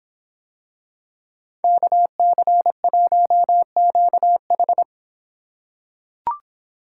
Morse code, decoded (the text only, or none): KC1QH